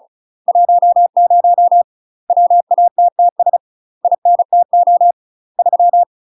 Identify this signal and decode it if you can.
Morse code, heard as a keyed tone, about 34 wpm, 700 Hz. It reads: 10 WATTS INTO 3
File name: UK MASTER QSOs TRAINING / 35WPM / UK MASTER QSOs 0024